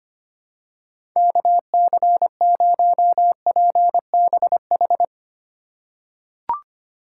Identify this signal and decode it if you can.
Morse code, heard as a keyed tone, about 25 wpm, 700 Hz. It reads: KC0PBH
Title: USA Call Signs - 6 Characters - 4x Character Spacing - 25wpm